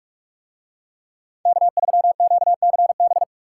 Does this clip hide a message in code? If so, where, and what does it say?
Yes: Morse code from 1.5 to 3.2 s, heard as a keyed tone, about 45 wpm, 700 Hz. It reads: K3QCB